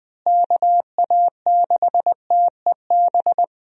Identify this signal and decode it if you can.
Morse code, heard as a keyed tone, about 20 wpm, 700 Hz. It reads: KA6TEB
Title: USA Call Signs - 6 Characters - Repeated 3x - 20wpm